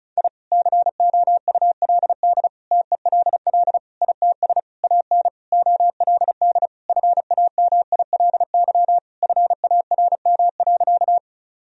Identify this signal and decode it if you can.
Morse code, heard as a keyed tone, about 34 wpm, 700 Hz. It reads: I COULD TELL ITS AN OLD FAMILY FARM.
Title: Sentences from Top 1000 Words - Review Part 2 - 35wpm